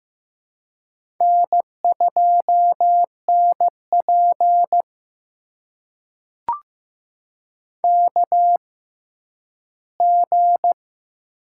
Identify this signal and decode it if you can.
Morse code, heard as a keyed tone, about 15 wpm, spaced as for 10 wpm, 700 Hz. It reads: N2NP KG
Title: USA Call Signs - 4 Characters - 6x Character Spacing - 15wpm